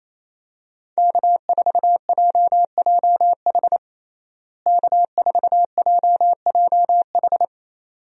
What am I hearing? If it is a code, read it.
Morse code, heard as a keyed tone, about 28 wpm, 700 Hz. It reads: K4JJH K4JJH
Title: USA Call Signs - 5 Characters - Repeated 2x - 28wpm